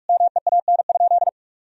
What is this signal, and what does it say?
Morse code, heard as a keyed tone, about 41 wpm, 700 Hz. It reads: MEAN?
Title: Sentences from Top 1000 Words - Review Part 1 - 45wpm